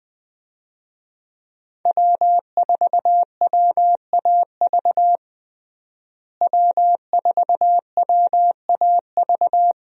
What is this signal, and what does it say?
Morse code, heard as a keyed tone, about 20 wpm, 700 Hz. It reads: W4WAV W4WAV
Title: USA Call Signs - 5 Characters - Repeated 2x - 20wpm